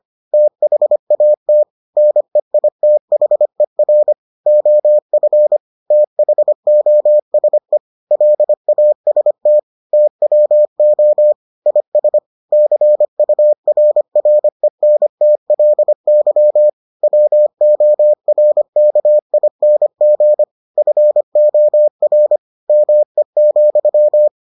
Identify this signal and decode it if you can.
Morse code, heard as a keyed tone, about 25 wpm, 600 Hz. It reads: THAT NEITHER OF THOSE LAST TWO IS CURRENTLY WORKING FOR ME,